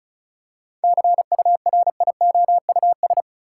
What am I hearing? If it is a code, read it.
Morse code, heard as a keyed tone, about 34 wpm, 700 Hz. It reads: CURIOUS
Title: Seven Letter Words - 35wpm